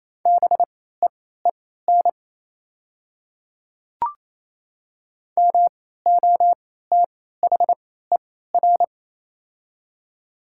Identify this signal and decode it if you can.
Morse code, heard as a keyed tone, about 28 wpm, spaced as for 16 wpm, 700 Hz. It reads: BEEN MOTHER
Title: 3x Character Spacing - Top 200 Words - 28wpm